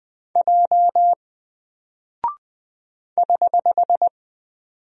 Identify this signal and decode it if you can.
Morse code, heard as a keyed tone, about 20 wpm, 700 Hz. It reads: J <HH>
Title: Single Character (Letter:Number:Punctuation:Prosign) Alt 20wpm